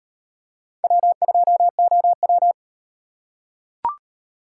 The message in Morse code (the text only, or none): W2OW